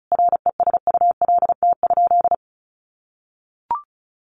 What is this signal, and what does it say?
Morse code, heard as a keyed tone, about 34 wpm, 700 Hz. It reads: RESULT?
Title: Sentences from Top 1500 Words 35wpm